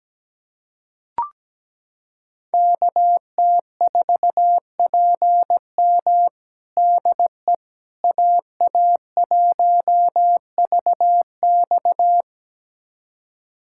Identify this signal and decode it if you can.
Morse code, heard as a keyed tone, about 17 wpm, 700 Hz. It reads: KT4PM DE AA1VX
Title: USA Callsign DE Callsign - 5 Characters - 17wpm